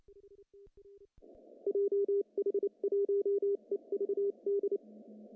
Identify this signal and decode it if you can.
Morse code, heard as a keyed tone, about 29 wpm, spaced as for 25 wpm, 410 Hz. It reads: JH1EVD
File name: CW / cqww-cw-k5tr